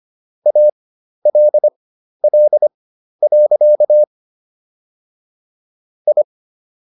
Morse code, heard as a keyed tone, about 25 wpm, spaced as for 12 wpm, 600 Hz. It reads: ALL. I